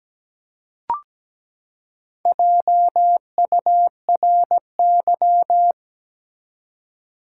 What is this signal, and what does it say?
Morse code, heard as a keyed tone, about 17 wpm, 700 Hz. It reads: JURY
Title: Four Letter Words - 17wpm